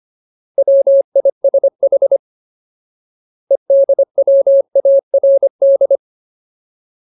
Morse code, heard as a keyed tone, about 25 wpm, 550 Hz. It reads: WISH EDWARD
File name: Ultimate Word List / 25WPM Ultimate